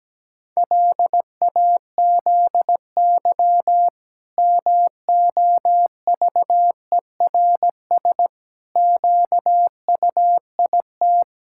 Morse code, heard as a keyed tone, about 17 wpm, 700 Hz. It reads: LAZY MOVERS QUIT
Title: Pangram Challenge v2 - 17wpm